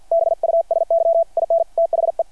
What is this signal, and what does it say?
Morse code, heard as a keyed tone, about 50 wpm, 690 Hz. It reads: BUSY IN THE
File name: archived / CRSnet